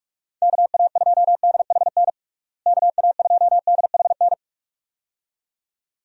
Morse code, heard as a keyed tone, about 46 wpm, 700 Hz. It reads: KA2DHN KA2DHN
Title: USA Call Signs - 6 Characters - Repeated 2x - 45wpm